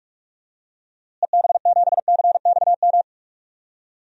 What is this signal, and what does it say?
Morse code, heard as a keyed tone, about 44 wpm, 700 Hz. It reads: EB7CXM
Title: DXCC-100 Call Signs - 6 Characters - Repeated 2x - 45wpm